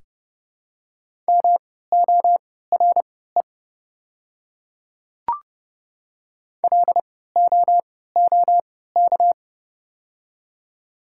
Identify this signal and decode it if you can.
Morse code, heard as a keyed tone, about 30 wpm, spaced as for 17 wpm, 690 Hz. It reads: MORE LOOK
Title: 3x Character Spacing - Top 100 Words - 30wpm